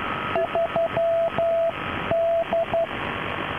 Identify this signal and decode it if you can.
Morse code, heard as a keyed tone, about 12 wpm, 670 Hz, with radio noise. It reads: 3D
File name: 3D- -0055utc-4200khz